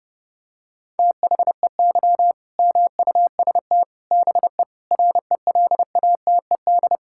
Morse code, heard as a keyed tone, about 30 wpm, 700 Hz. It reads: THEY MUST BE RELATED
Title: Sentences from Top 1000 Words - Review Part 5 - 30wpm